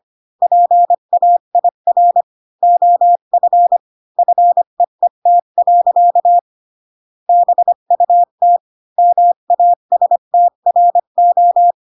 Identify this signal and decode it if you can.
Morse code, heard as a keyed tone, about 25 wpm, 700 Hz. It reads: PAIR OF FEET. BUT MASTRO
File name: Pinocchio / Pinocchio ch8 20 25 ik1hge